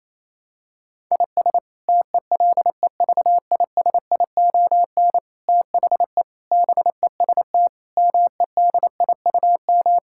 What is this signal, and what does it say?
Morse code, heard as a keyed tone, about 28 wpm, 700 Hz. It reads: IS TELEVISION THE BEST MEDIUM
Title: Sentences from Top 1000 Words - Review Part 6 - 28wpm